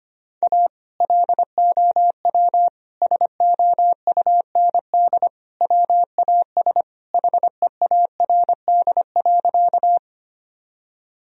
Morse code, heard as a keyed tone, about 25 wpm, 700 Hz. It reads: A LOW SOUND WAS HEARD.
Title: Sentences from Top 1000 Words - Review Part 1 - 25wpm